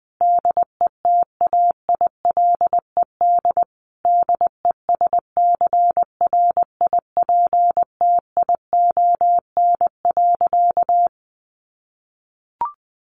Morse code, heard as a keyed tone, about 20 wpm, 700 Hz. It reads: DETAILED DESCRIPTION.
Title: Sentences from Top 1500 Words 20wpm